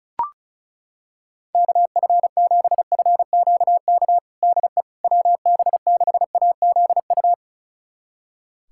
Morse code, heard as a keyed tone, about 34 wpm, 700 Hz. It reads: KF7FQK DE WB6AZU
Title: USA Callsign DE Callsign - 6 Characters - 35wpm